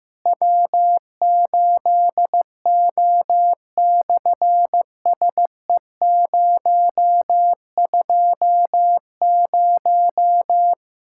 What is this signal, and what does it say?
Morse code, heard as a keyed tone, about 15 wpm, 700 Hz. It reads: W8O/SE020
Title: SOTA QSO Practice 15wpm